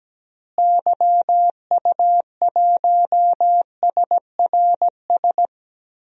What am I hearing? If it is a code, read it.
Morse code, heard as a keyed tone, about 17 wpm, 700 Hz. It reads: YU1SRS
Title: DXCC-100 Call Signs - 6 Characters - Repeated 2x - 17wpm